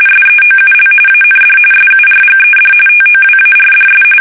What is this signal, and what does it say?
RTTY, about 45.45 baud. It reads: SEOUL AND SURGED TOWARDS
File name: fsk850 / battleshipnj